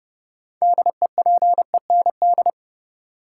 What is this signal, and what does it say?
Morse code, heard as a keyed tone, about 30 wpm, 700 Hz. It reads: DEPEND
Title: Sentences from Top 1300 Words - New Word Drill - 30wpm